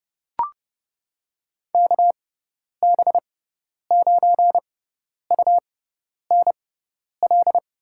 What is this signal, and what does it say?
Morse code, heard as a keyed tone, about 30 wpm, spaced as for 10 wpm, 700 Hz. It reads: KB9UNL
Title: USA Call Signs - 6 Characters - 6x Character Spacing - 30wpm